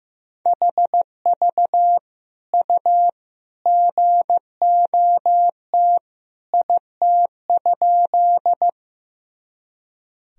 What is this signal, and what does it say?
Morse code, heard as a keyed tone, about 16 wpm, 700 Hz. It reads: HV U GOT IT?
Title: Sentences from Top 300 Words and Abbreviations 15wpm